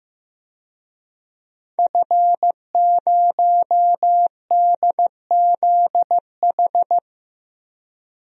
Morse code, heard as a keyed tone, about 15 wpm, 700 Hz. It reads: F0DZH
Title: DXCC-100 Call Signs - 5 Characters - Repeated 2x - 15wpm